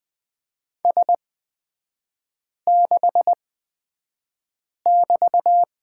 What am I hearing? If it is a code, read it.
Morse code, heard as a keyed tone, about 20 wpm, spaced as for 5 wpm, 700 Hz. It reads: S6<BT>
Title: 255 Lesson 41e - Period - Three Characters 20wpm